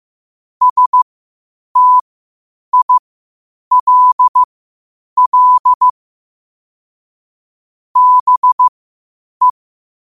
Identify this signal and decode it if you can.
Morse code, heard as a keyed tone, about 15 wpm, spaced as for 8 wpm, 1000 Hz. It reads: STILL BE
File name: qotdfiles / 8wpm